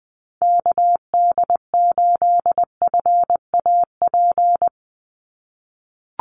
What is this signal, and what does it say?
Morse code, heard as a keyed tone, about 20 wpm, 700 Hz. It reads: KD8FAP
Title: USA Call Signs 20wpm